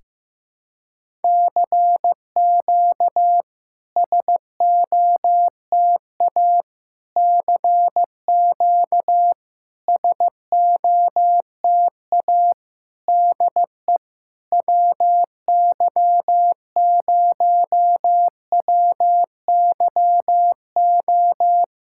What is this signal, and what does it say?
Morse code, heard as a keyed tone, about 15 wpm, 700 Hz. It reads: CQ SOTA CQ SOTA DE WY0WYO
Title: SOTA QSO Practice 15wpm